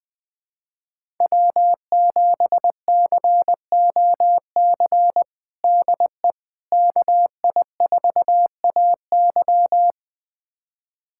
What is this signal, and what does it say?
Morse code, heard as a keyed tone, about 20 wpm, 700 Hz. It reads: W7COC DE KI4AY
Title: USA Callsign DE Callsign - 5 Characters - 20wpm